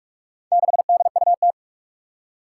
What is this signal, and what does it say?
Morse code, heard as a keyed tone, about 46 wpm, 700 Hz. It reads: 6DUT
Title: Four Challenging Letter-Number Combinations 45wpm